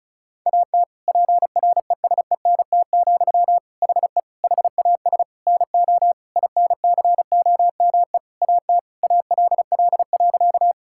Morse code, heard as a keyed tone, about 36 wpm, 700 Hz. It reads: AT PRESENT, HE HAS NO INCOME AT ALL.
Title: Sentences from Top 1000 Words - Review Part 5 - 35wpm